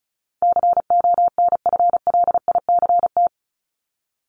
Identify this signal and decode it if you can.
Morse code, heard as a keyed tone, about 34 wpm, 700 Hz. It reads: CONFLICT
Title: Top 1001-1250 Words 35wpm